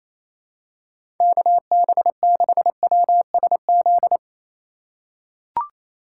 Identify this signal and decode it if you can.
Morse code, heard as a keyed tone, about 27 wpm, 700 Hz. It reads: KB6WSZ
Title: USA Call Signs - 6 Characters - 4x Character Spacing - 28wpm